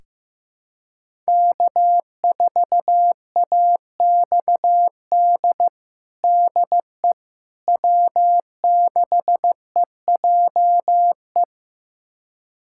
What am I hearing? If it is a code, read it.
Morse code, heard as a keyed tone, about 15 wpm, 700 Hz. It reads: K4AXD DE W6EJE
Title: USA Callsign DE Callsign 15wpm